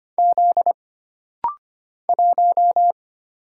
Morse code, heard as a keyed tone, about 25 wpm, 700 Hz. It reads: Z 1
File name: Single Character (Letter:Number:Punctuation:Prosign) - Rapid-Fire - 25wpm